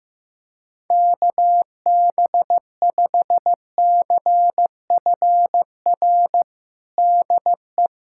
Morse code, heard as a keyed tone, about 15 wpm, 700 Hz. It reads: KB5CFR DE
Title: USA Callsign DE Callsign 15wpm